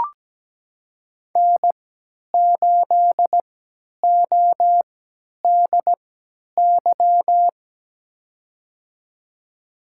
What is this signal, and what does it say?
Morse code, heard as a keyed tone, about 17 wpm, spaced as for 10 wpm, 700 Hz. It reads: N8ODY
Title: USA Call Signs - 5 Characters - 3x Character Spacing - 17wpm